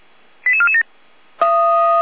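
Five-tone selective call (ZVEI), heard as quick runs of five tones, tones of about 70 ms. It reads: 88447
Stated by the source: ZVEI five-tone selective call 88352 received by SDR.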